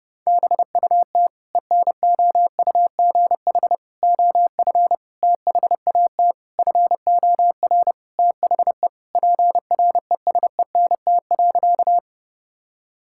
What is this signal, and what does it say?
Morse code, heard as a keyed tone, about 30 wpm, 700 Hz. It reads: BUT ENOUGH OF THAT FOR THE PRESENT.
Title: Sentences from Top 1000 Words - Review Part 5 - 30wpm